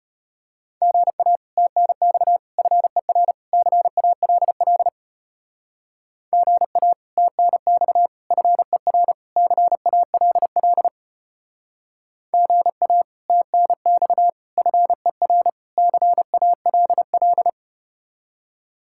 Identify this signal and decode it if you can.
Morse code, heard as a keyed tone, about 33 wpm, 700 Hz. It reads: GA TNX FER CALL GA TNX FER CALL GA TNX FER CALL